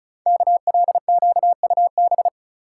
Morse code, heard as a keyed tone, about 36 wpm, 690 Hz. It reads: KLQUB
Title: Code Groups - 5 Letter:Numbers - 35wpm